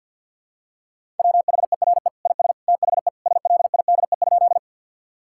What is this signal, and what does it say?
Morse code, heard as a keyed tone, about 51 wpm, 700 Hz. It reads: WHERE IS THE SLIDE?
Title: Sentences from Top 1600 Words 50wpm